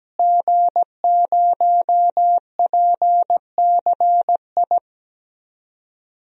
Morse code, heard as a keyed tone, about 17 wpm, 700 Hz. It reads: G0PCI
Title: DXCC-100 Call Signs - 5 Characters - Repeated 3x - 17wpm